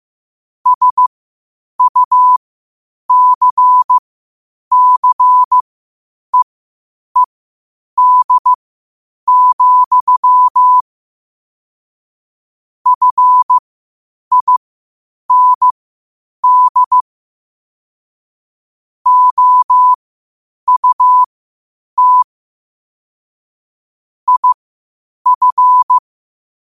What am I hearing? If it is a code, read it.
Morse code, heard as a keyed tone, about 15 wpm, spaced as for 8 wpm, 1000 Hz. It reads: SUCCEED, FIND OUT IF
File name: qotdfiles / 8wpm